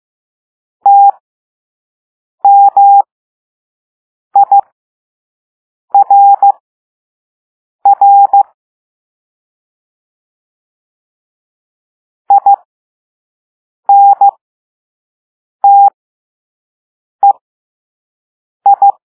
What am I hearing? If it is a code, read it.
Morse code, heard as a keyed tone, about 15 wpm, spaced as for 5 wpm, 800 Hz. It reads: TMIRR INTEI